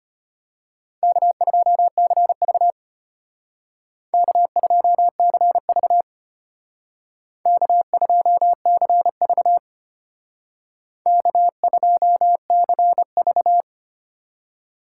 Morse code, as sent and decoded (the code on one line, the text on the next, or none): -.- ..--- -.-. ...- / -.- ..--- -.-. ...- / -.- ..--- -.-. ...- / -.- ..--- -.-. ...-
K2CV K2CV K2CV K2CV